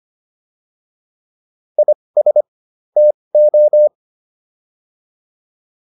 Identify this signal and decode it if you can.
Morse code, heard as a keyed tone, about 25 wpm, spaced as for 20 wpm, 600 Hz. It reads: IS TO